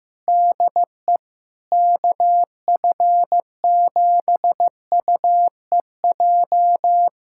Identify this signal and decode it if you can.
Morse code, heard as a keyed tone, about 15 wpm, 700 Hz. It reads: DE KF7UEJ
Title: K1USN SST QSO Practice 15wpm